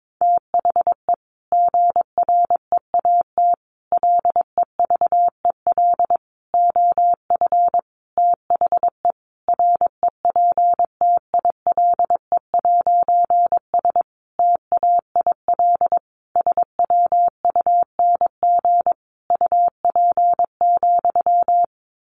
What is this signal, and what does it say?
Morse code, heard as a keyed tone, about 22 wpm, 700 Hz. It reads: THE GREAT LEVEL OF THE REPTILE'S TAIL SWUNG UP,
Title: A Sound Of Thunder 22wpm